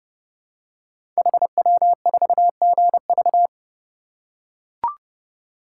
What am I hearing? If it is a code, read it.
Morse code, heard as a keyed tone, about 30 wpm, 700 Hz. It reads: HW4GV
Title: Code Groups - 5 Letter:Numbers - 30wpm